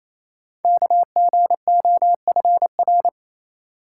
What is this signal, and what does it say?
Morse code, heard as a keyed tone, about 28 wpm, 700 Hz. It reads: KGOFR
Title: Code Groups - 5 Letters - 28wpm